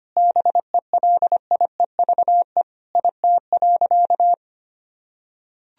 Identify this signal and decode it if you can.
Morse code, heard as a keyed tone, about 25 wpm, 700 Hz. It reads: BELIEVE IT.
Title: Sentences from Top 1000 Words - Review Part 3 - 25wpm